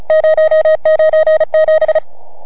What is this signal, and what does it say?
Morse code, heard as a keyed tone, about 32 wpm, 680 Hz. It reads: 097